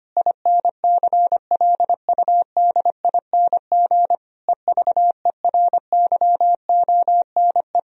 Morse code, heard as a keyed tone, about 25 wpm, 700 Hz. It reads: INCLUDING EVERYONE
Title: Sentences from Top 1000 Words - Review Part 4 - 25wpm